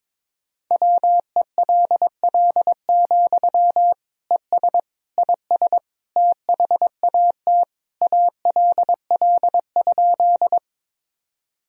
Morse code, heard as a keyed tone, about 22 wpm, 700 Hz. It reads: WELL, ES IS THAT ALL?
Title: Sentences from Top 200 Words and Abbreviations 22wpm